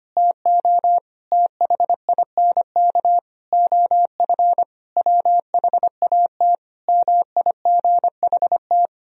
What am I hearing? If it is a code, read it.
Morse code, heard as a keyed tone, about 25 wpm, 700 Hz. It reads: TO THINK OF WHAT MIGHT
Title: Sentences from Top 400 Words and Abbreviations 25wpm